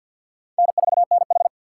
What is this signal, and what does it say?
Morse code, heard as a keyed tone, about 49 wpm, 700 Hz. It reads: N4NH
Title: USA Call Signs - 4 Characters - 6x Character Spacing - 50wpm